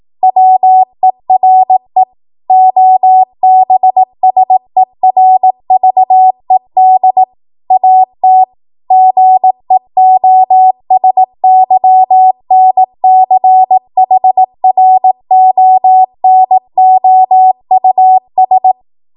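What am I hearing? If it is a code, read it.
Morse code, heard as a keyed tone, about 18 wpm, 750 Hz. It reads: WERE OBSERVED AT GEOSYNCHRONOUS
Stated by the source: ARRL 18 wpm Morse code practice archive transmission.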